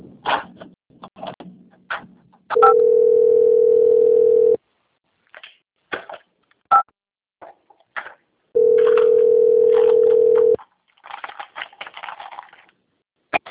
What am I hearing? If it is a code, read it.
Ringback tone (North America), 440 + 480 Hz.